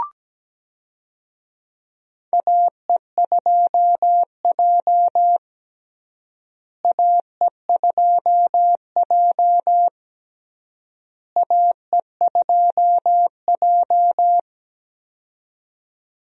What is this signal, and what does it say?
Morse code, heard as a keyed tone, about 17 wpm, 700 Hz. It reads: AE2J AE2J AE2J
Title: USA Call Signs - 4 Characters - Repeated 3x - 17wpm